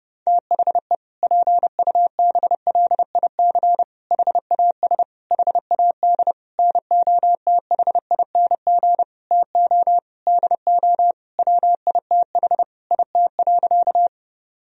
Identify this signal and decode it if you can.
Morse code, heard as a keyed tone, about 30 wpm, 700 Hz. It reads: THE PUBLIC HAS HAD NOTHING TO DO WITH IT.
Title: Sentences from Top 1000 Words - Review Part 3 - 30wpm